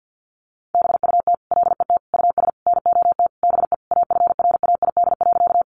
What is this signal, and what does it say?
Morse code, heard as a keyed tone, about 50 wpm, 700 Hz. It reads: BUT LET US NOT BE AFRAID.
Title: Sentences from Top 1500 Words 50wpm